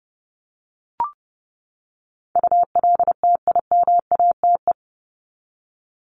Morse code, heard as a keyed tone, about 30 wpm, 700 Hz. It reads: ULTIMATE
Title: LY Suffix - Top 100 Words with Base Words 30wpm